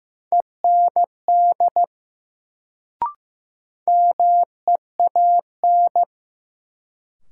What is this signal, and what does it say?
Morse code, heard as a keyed tone, about 15 wpm, 700 Hz. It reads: END MEAN
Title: Sentences from Top 200 Words - New Word Drill - 15wpm